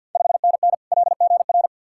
Morse code, heard as a keyed tone, about 52 wpm, 700 Hz. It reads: 5NN LGR
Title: Ontario QSO Party Exchange 50wpm